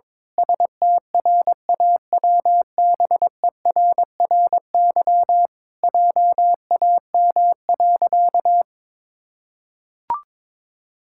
Morse code, heard as a keyed tone, about 22 wpm, 700 Hz. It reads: STRAWBERRY JAM.
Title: Pangram Challenge v2 - 22wpm